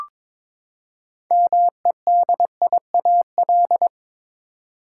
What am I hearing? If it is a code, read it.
Morse code, heard as a keyed tone, about 21 wpm, 700 Hz. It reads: MEDIAL
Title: Practice QSO - Exchange 3 - More Info - 22wpm